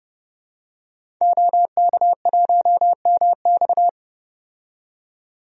Morse code, heard as a keyed tone, about 30 wpm, 700 Hz. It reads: OK1MX
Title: DXCC-100 Call Signs - 5 Characters - 30wpm